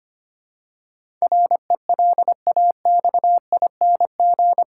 Morse code, heard as a keyed tone, about 25 wpm, 700 Hz. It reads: RELAXING
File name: 8 Letter Words - First 3 Letters Given - 25wpm